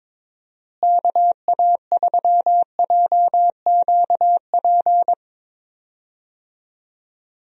Morse code, heard as a keyed tone, about 22 wpm, 700 Hz. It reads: KA3JQP
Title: USA Call Signs - 6 Characters - Repeated 3x - 22wpm